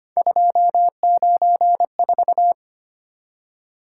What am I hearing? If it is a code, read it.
Morse code, heard as a keyed tone, about 25 wpm, 700 Hz. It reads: 294
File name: Three Letter-Number Combinations - 25wpm